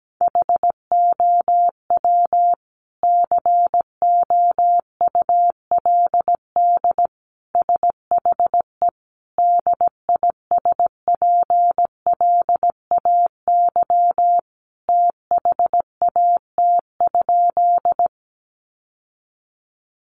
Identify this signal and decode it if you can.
Morse code, heard as a keyed tone, about 17 wpm, 700 Hz. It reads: HOW COULD SHE DISPLAY THAT?
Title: Sentences from Top 1200 Words 17wpm